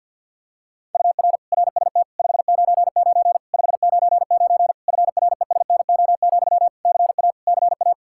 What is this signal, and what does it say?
Morse code, heard as a keyed tone, about 49 wpm, 700 Hz. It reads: UR RST 599 599 FRESNO, CA CA
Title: Practice QSO - Exchange 1 - Beginning - 50wpm